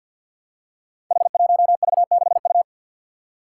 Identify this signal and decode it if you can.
Morse code, heard as a keyed tone, about 52 wpm, 700 Hz. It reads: H1V6U
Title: Five Challenging Letter-Number Combinations 50wpm